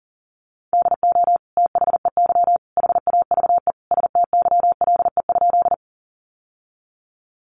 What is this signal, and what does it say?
Morse code, heard as a keyed tone, about 40 wpm, 700 Hz. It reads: DO THEY HAVE STYLE?
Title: Sentences from Top 800 Words 40wpm